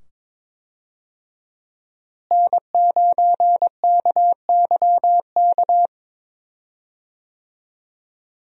Morse code, heard as a keyed tone, about 22 wpm, 700 Hz. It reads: N9KYK
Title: USA Call Signs - 5 Characters - Repeated 2x - 22wpm